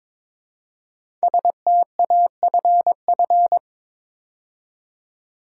Morse code, heard as a keyed tone, about 22 wpm, 700 Hz. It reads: STAFF
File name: 5 Letter Words - First 3 Letters Given - 22wpm